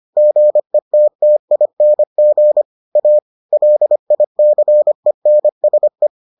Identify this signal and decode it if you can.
Morse code, heard as a keyed tone, about 25 wpm, 600 Hz. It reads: GETTING A LICENSE